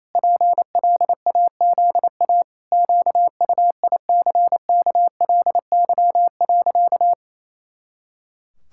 Morse code, heard as a keyed tone, about 28 wpm, 700 Hz. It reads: PLAZA QUICKLY.
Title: Pangram Challenge v2 - 28wpm